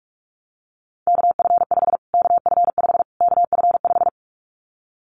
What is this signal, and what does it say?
Morse code, heard as a keyed tone, about 44 wpm, 700 Hz. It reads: KF5 KF5 KF5
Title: Callsign Prefixes for Top 25 DXCC - Repeated 3x - 45wpm